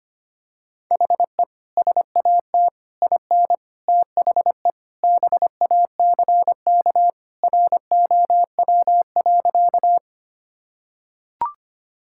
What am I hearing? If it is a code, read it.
Morse code, heard as a keyed tone, about 25 wpm, 700 Hz. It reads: HE SAT IN THE BACK ROW.
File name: practice / 25wpm / Sentences with 3-Word Phrases - 25wpm